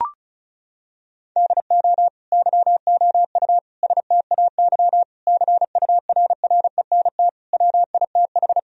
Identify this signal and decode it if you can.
Morse code, heard as a keyed tone, about 34 wpm, 700 Hz. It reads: DO YOU STAY CURRENT WITH
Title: Sentences from Top 1000 Words - Review Part 4 - 35wpm